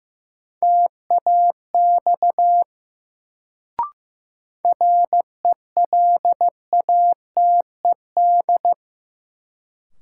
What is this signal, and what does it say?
Morse code, heard as a keyed tone, about 15 wpm, 700 Hz. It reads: TAX RELATED
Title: Sentences from Top 1000 Words - Review Part 5 - New Word Drill - 15wpm